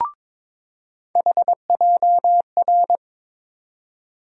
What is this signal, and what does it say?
Morse code, heard as a keyed tone, about 21 wpm, 700 Hz. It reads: HJR
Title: USA Call Sign Postfixes - 3 Letters 22wpm